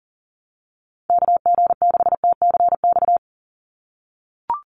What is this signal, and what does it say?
Morse code, heard as a keyed tone, about 38 wpm, 700 Hz. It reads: KG6TCX
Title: USA Call Signs 40wpm